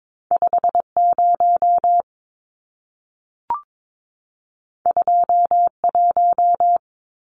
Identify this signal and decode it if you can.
Morse code, heard as a keyed tone, about 22 wpm, 700 Hz. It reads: 50 21
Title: Numbers - 2-Digit 22wpm